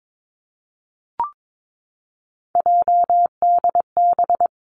Morse code, heard as a keyed tone, about 22 wpm, 700 Hz. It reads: JDB
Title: Three Challenging Letter-Number Combinations 22wpm